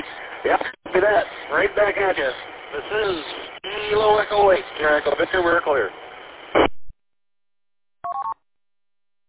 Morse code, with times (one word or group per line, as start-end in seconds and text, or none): none